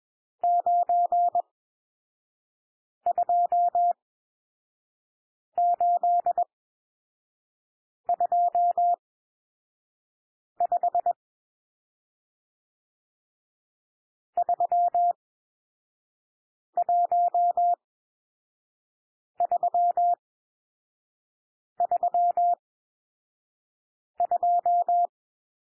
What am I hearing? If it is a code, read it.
Morse code, heard as a keyed tone, about 21 wpm, spaced as for 5 wpm, 700 Hz. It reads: 92825 31332
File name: cwmp3 / N1 SOUND NUMBERS 21 wpm